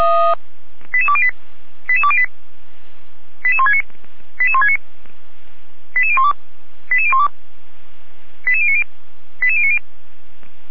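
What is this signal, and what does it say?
Five-tone selective call (ZVEI), heard as quick runs of five tones, tones of about 70 ms. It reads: 88208 88208 88169 88169 88012 88012 88090 88090
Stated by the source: ZVEI five-tone call received by SDR, from the feuerwehr-orlamuende.org audio archive.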